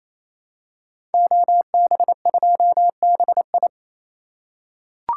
Morse code, heard as a keyed tone, about 29 wpm, 700 Hz. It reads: OB2BI
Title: Code Groups - 5 Letter:Numbers - 28wpm